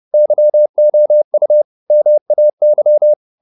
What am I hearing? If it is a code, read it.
Morse code, heard as a keyed tone, about 30 wpm, 600 Hz. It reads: YOU MAY